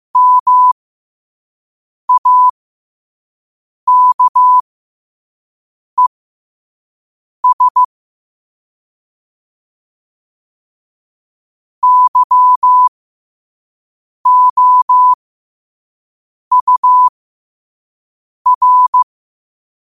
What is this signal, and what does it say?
Morse code, heard as a keyed tone, about 15 wpm, spaced as for 5 wpm, 1000 Hz. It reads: MAKES YOUR
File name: qotdfiles / 5wpm